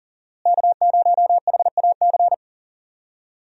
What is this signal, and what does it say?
Morse code, heard as a keyed tone, about 40 wpm, 700 Hz. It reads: K0HAC
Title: USA Call Signs - 5 Characters - Repeated 3x - 40wpm